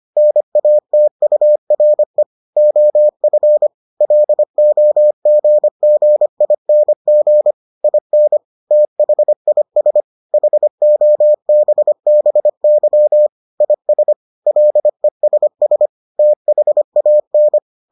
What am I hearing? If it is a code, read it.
Morse code, heard as a keyed tone, about 25 wpm, 600 Hz. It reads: NATURE OF LOGGING IN THIS HOBBY IS LESS THAN